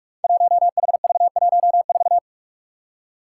Morse code, heard as a keyed tone, about 44 wpm, 700 Hz. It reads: 1HV14
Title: Five Challenging Letter-Number Combinations 45wpm